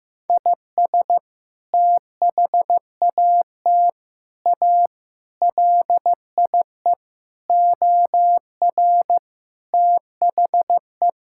Morse code, heard as a keyed tone, about 16 wpm, 700 Hz. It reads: IS THAT A LIE OR THE
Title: Sentences from Top 1600 Words 15wpm